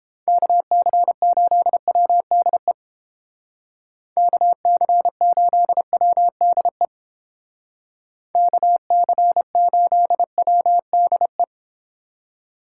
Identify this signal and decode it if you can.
Morse code, heard as a keyed tone, about 30 wpm, 700 Hz. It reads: KC8WDE KC8WDE KC8WDE